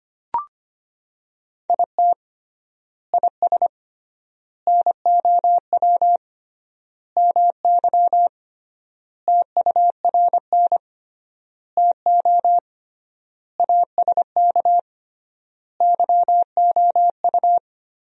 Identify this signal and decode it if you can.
Morse code, heard as a keyed tone, about 25 wpm, 700 Hz. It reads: IT IS NOW MY TURN TO ASK YOU
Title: 3x Word-Spacing - Sentences from Top 200 Words - 25wpm